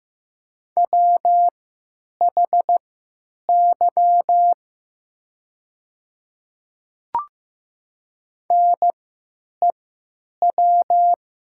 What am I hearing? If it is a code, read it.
Morse code, heard as a keyed tone, about 15 wpm, spaced as for 9 wpm, 690 Hz. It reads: WHY NEW
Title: 3x Character Spacing - Top 300 Words - 15wpm